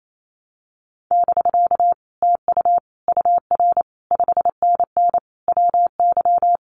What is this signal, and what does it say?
Morse code, heard as a keyed tone, about 28 wpm, 700 Hz. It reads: <BK> TU UR 5NN WY
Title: Common QSO Phrases Encore 28wpm